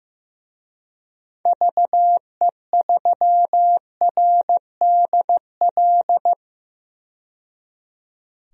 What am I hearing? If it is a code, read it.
Morse code, heard as a keyed tone, about 15 wpm, 700 Hz. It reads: VE3RDL
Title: DXCC-100 Call Signs - 6 Characters - 15wpm